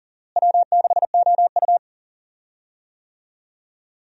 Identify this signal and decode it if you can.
Morse code, heard as a keyed tone, about 39 wpm, 700 Hz. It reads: W6OU
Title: USA Call Signs - 4 Characters - Repeated 3x - 40wpm